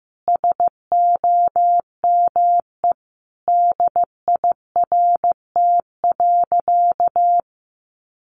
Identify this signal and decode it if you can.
Morse code, heard as a keyed tone, about 15 wpm, 700 Hz. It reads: SOME DIRT.
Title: A Sound Of Thunder 15wpm